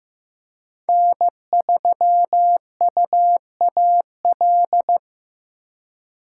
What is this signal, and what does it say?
Morse code, heard as a keyed tone, about 15 wpm, 700 Hz. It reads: N3UAL
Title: USA Call Signs - 5 Characters - 4x Character Spacing - 15wpm